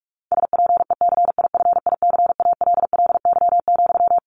Morse code, heard as a keyed tone, about 44 wpm, 700 Hz. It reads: SPECIFICALLY,
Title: Sentences from Top 1100 Words 45wpm